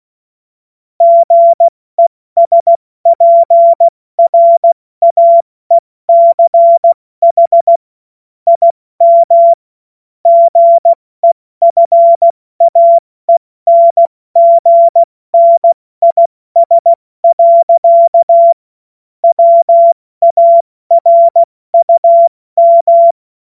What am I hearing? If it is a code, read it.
Morse code, heard as a keyed tone, about 16 wpm, 680 Hz. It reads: GESPRAECH IM GEFAENGNIS. WARUM
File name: Morsekurs / Witze / Witz 01. März 2023 70 F80